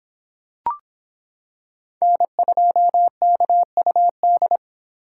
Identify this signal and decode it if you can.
Morse code, heard as a keyed tone, about 25 wpm, 700 Hz. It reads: N2KUD